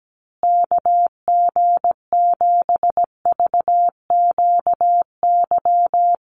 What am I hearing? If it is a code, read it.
Morse code, heard as a keyed tone, about 17 wpm, 700 Hz. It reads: KG7VQY
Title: USA Call Signs Repeated 3x 17wpm